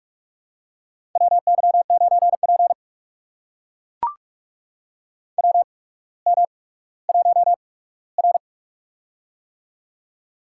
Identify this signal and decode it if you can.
Morse code, heard as a keyed tone, about 44 wpm, spaced as for 19 wpm, 700 Hz. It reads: WY9P WM1R
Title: USA Call Signs - 4 Characters - 8x Character Spacing - 45wpm